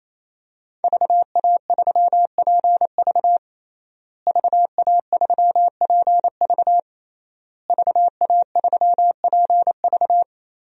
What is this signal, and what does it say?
Morse code, heard as a keyed tone, about 27 wpm, 700 Hz. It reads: VA3PV VA3PV VA3PV
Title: DXCC-100 Call Signs - 5 Characters - Repeated 3x - 28wpm